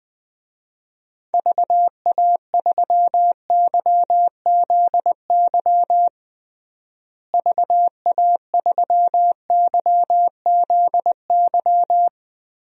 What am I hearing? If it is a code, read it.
Morse code, heard as a keyed tone, about 20 wpm, 700 Hz. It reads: VA3YZY VA3YZY
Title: DXCC-100 Call Signs - 6 Characters - Repeated 2x - 20wpm